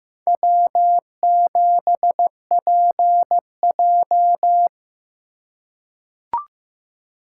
Morse code, heard as a keyed tone, about 15 wpm, 700 Hz. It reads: W7PJ
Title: USA Call Signs - 4 Characters - 3x Character Spacing - 15wpm